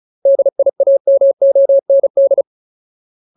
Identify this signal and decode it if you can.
Morse code, heard as a keyed tone, about 34 wpm, 550 Hz. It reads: DIAMOND
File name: Ultimate Word List / 35WPM Ultimate